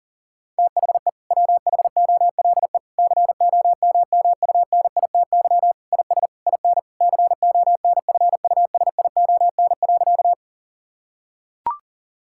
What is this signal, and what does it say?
Morse code, heard as a keyed tone, about 40 wpm, 700 Hz. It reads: THE WHOLE COMMUNITY IS IN CONFUSION.
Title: Sentences from Top 1600 Words 40wpm